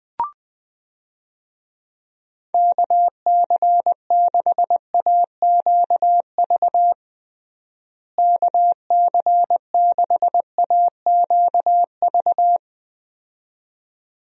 Morse code, heard as a keyed tone, about 20 wpm, 700 Hz. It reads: KC6AQV KC6AQV
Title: USA Call Signs - 6 Characters - Repeated 2x - 20wpm